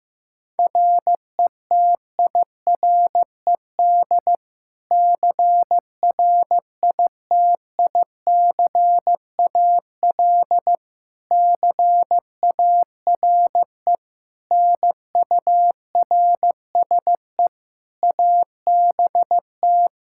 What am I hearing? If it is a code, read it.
Morse code, heard as a keyed tone, about 15 wpm, 700 Hz. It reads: RETIRED CRITICAL CARE NURSE ABT
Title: Practice QSO - Exchange 3 - More Info - 15wpm